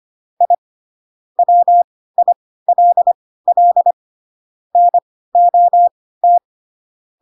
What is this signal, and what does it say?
Morse code, heard as a keyed tone, about 25 wpm, spaced as for 16 wpm, 700 Hz. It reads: I WILL NOT
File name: Pinocchio / Pinocchio ch7 10 25 ik1hge